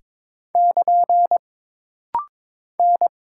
Morse code, heard as a keyed tone, about 22 wpm, 700 Hz. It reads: <KN> N
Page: This rapid-fire timing is geared to push your ICR (Instant Character Recognition) skills.